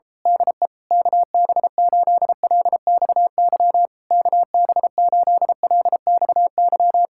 Morse code, heard as a keyed tone, about 27 wpm, 700 Hz. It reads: DE KB8LXY KB8LXY
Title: Speed-Racing Common QSO Phrases - 28wpm